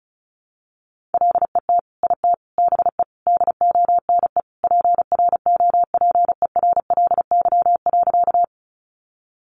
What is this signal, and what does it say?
Morse code, heard as a keyed tone, about 34 wpm, 700 Hz. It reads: LET IT BE DONE PROPERLY.
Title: Sentences from Top 1100 Words 35wpm